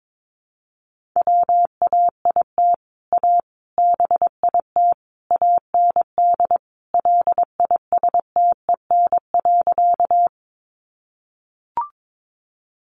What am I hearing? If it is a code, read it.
Morse code, heard as a keyed tone, about 22 wpm, 700 Hz. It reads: WAIT A BIT AND LISTEN.
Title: Sentences from Top 900 Words 22wpm